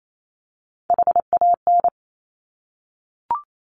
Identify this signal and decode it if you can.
Morse code, heard as a keyed tone, about 28 wpm, 700 Hz. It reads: HAN
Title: Top Three Letter Combinations in Words 28wpm